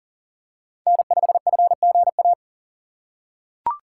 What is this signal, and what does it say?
Morse code, heard as a keyed tone, about 40 wpm, 700 Hz. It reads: N5FGA
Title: USA Call Signs - 4x Character Spacing - 40wpm